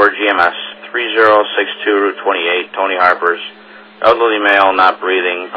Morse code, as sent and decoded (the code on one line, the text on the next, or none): none